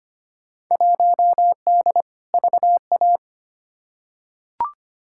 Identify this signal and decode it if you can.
Morse code, heard as a keyed tone, about 25 wpm, 700 Hz. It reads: 1D VA
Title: Field Day Exchanges - 25wpm